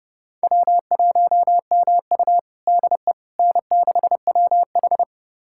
Morse code, heard as a keyed tone, about 30 wpm, 700 Hz. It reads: W1MU DE N6WH
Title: USA Callsign DE Callsign - 4 Characters - 30wpm